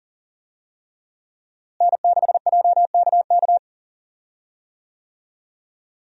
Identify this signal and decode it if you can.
Morse code, heard as a keyed tone, about 40 wpm, 700 Hz. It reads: N6JKK
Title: USA Call Signs - 5 Characters - Repeated 3x - 40wpm